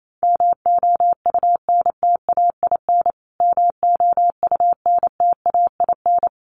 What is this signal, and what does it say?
Morse code, heard as a keyed tone, about 28 wpm, 700 Hz. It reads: MOUNTAIN MOUNTAIN
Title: Top 500 Words Repeated 28wpm